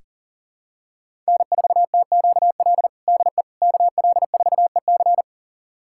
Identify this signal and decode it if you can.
Morse code, heard as a keyed tone, about 40 wpm, 700 Hz. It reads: N4TQL DE KL4EC
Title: USA Callsign DE Callsign - 5 Characters - 40wpm